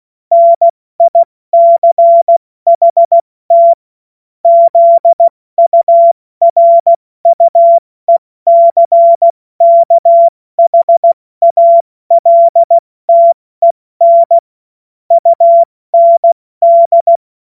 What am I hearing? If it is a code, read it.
Morse code, heard as a keyed tone, about 16 wpm, 680 Hz. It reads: NICHT ZURUECKHALTEN UND
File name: Morsekurs / Witze / Witz 29. Juni 2021, 70 F80